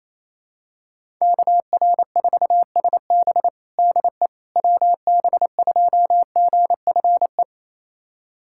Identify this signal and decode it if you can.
Morse code, heard as a keyed tone, about 28 wpm, 700 Hz. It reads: KR4SB DE WB2GFE
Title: USA Callsign DE Callsign 28wpm